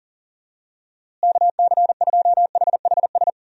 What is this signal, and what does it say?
Morse code, heard as a keyed tone, about 40 wpm, 700 Hz. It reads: KC2HHS
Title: USA Call Signs - 6 Characters - Repeated 2x - 40wpm